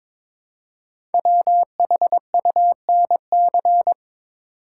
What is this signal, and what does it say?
Morse code, heard as a keyed tone, about 22 wpm, 700 Hz. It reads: WHUNC
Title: Code Groups - 5 Letters - 22wpm